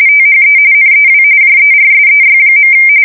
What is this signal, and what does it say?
RTTY, about 45.45 baud. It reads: THE NEW JERSEY.
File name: fsk170 / USS-New-Jersey